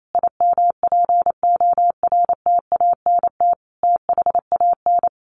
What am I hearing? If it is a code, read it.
Morse code, heard as a keyed tone, about 28 wpm, 700 Hz. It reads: IMPORTANT THAN
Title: Sentences from Top 600 Words 28wpm